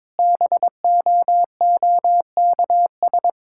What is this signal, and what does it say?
Morse code, heard as a keyed tone, about 22 wpm, 700 Hz. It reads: BOOKS
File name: practice / 22wpm / Sentences with 8-Word Phrases - 22wpm